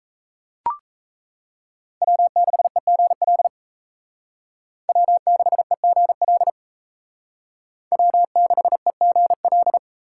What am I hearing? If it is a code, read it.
Morse code, heard as a keyed tone, about 36 wpm, 700 Hz. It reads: W6EGL W6EGL W6EGL